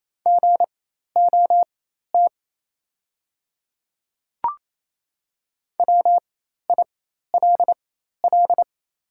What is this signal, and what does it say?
Morse code, heard as a keyed tone, about 28 wpm, spaced as for 13 wpm, 690 Hz. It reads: GOT WILL
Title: 4x Character Spacing - Top 300 Words - 28wpm